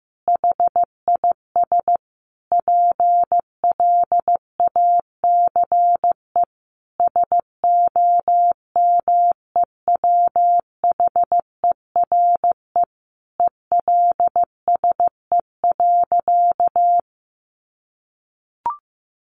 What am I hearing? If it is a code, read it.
Morse code, heard as a keyed tone, about 15 wpm, 700 Hz. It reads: HIS PLACE SOMEWHERE ELSE.
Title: Sentences from Top 1400 Words 15wpm